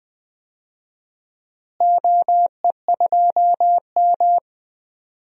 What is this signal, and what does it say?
Morse code, heard as a keyed tone, about 20 wpm, 700 Hz. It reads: OE2M
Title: DXCC-100 Call Signs - 4 Characters - Repeated 3x - 20wpm